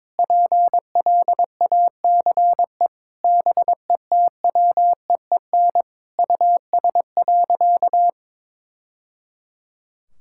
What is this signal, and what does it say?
Morse code, heard as a keyed tone, about 22 wpm, 700 Hz. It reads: PLACE BETWEEN US.
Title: Sentences from Top 1000 Words - Review Part 6 - 22wpm